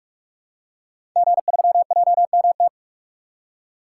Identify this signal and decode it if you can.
Morse code, heard as a keyed tone, about 44 wpm, 690 Hz. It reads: G3JMT
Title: Code Groups - 5 Letter:Numbers - 45wpm